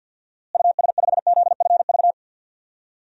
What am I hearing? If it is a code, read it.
Morse code, heard as a keyed tone, about 52 wpm, 700 Hz. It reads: US5ZFV
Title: DXCC-100 Call Signs - 6 Characters - Repeated 3x - 50wpm